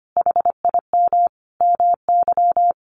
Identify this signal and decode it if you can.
Morse code, heard as a keyed tone, about 25 wpm, 700 Hz. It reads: HIM MY
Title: Sentences from Top 1400 Words 25wpm